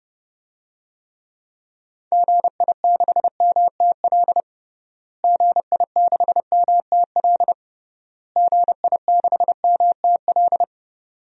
Morse code, heard as a keyed tone, about 30 wpm, 700 Hz. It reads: GI6MTL GI6MTL GI6MTL
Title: DXCC-100 Call Signs - 6 Characters - Repeated 3x - 30wpm